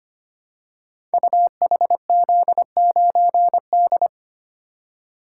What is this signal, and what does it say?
Morse code, heard as a keyed tone, about 25 wpm, 700 Hz. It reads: UHZ9D
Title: Code Groups - 5 Letter:Numbers - 25wpm